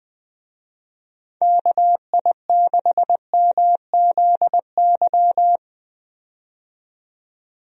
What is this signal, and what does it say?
Morse code, heard as a keyed tone, about 20 wpm, 700 Hz. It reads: KI6MZY
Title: USA Call Signs - 6 Characters - 20wpm